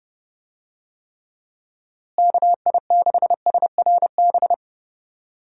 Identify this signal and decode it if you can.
Morse code, heard as a keyed tone, about 29 wpm, 700 Hz. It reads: KI6SRB
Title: USA Call Signs - 6 Characters - Repeated 3x - 30wpm